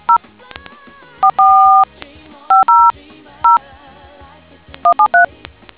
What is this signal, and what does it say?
DTMF, heard as telephone touch tones, tones of about 130 ms. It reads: 0 44 500 173